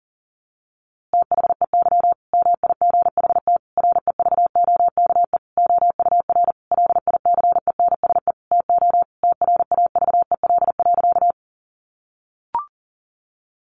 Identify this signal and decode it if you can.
Morse code, heard as a keyed tone, about 37 wpm, 700 Hz. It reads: THEY MIGHT REVOKE OUR LICENSE TO TRAVEL.
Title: A Sound Of Thunder 40wpm